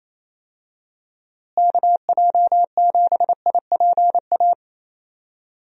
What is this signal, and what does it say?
Morse code, heard as a keyed tone, about 28 wpm, 700 Hz. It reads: KJ7IPA
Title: USA Call Signs - 6 Characters - Repeated 2x - 28wpm